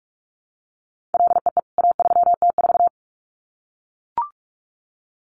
Morse code, heard as a keyed tone, about 44 wpm, 700 Hz. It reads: LEE A3T4
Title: CWT Exchange 45wpm